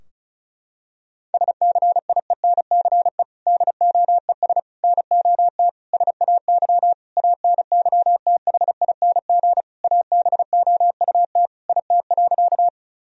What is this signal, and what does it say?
Morse code, heard as a keyed tone, about 36 wpm, 700 Hz. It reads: SCIENCE DOES NOT SAY ANYTHING ABOUT IT.
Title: Sentences from Top 1000 Words - Review Part 2 - 35wpm